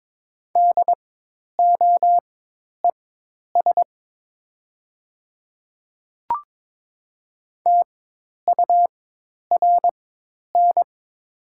Morse code, heard as a keyed tone, about 22 wpm, spaced as for 10 wpm, 700 Hz. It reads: DOES TURN
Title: 4x Character Spacing - Top 200 Words - 22wpm